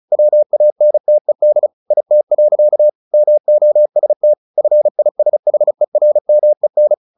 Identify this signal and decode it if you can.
Morse code, heard as a keyed tone, about 34 wpm, 600 Hz. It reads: WANTED IT. MOST FISHERMEN
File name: the old man and the sea / the old man and the sea 1 35wpm